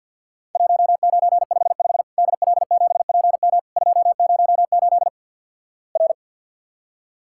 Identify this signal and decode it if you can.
Morse code, heard as a keyed tone, about 48 wpm, 700 Hz. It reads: 1955 DL7PM 208 R
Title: Worked All Europe DX Contest - QTC Exchange - 50wpm